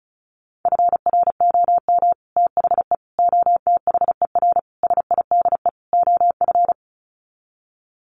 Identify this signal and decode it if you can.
Morse code, heard as a keyed tone, about 34 wpm, 700 Hz. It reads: FROM THE OTHER SIDE OF
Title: Sets of 5 Words Encore 35wpm